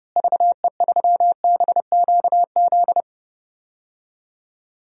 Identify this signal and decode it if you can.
Morse code, heard as a keyed tone, about 32 wpm, 700 Hz. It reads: VE3BQZ
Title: DXCC-100 Call Signs - 6 Characters - Repeated 3x - 30wpm